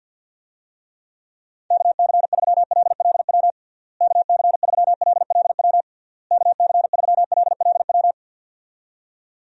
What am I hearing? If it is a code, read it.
Morse code, heard as a keyed tone, about 53 wpm, 700 Hz. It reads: KC3LLW KC3LLW KC3LLW
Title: USA Call Signs - 6 Characters - Repeated 3x - 50wpm